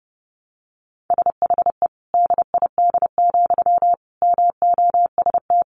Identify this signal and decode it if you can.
Morse code, heard as a keyed tone, about 30 wpm, 700 Hz. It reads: SHE DID, MOST
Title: Sentences from Top 1300 Words 30wpm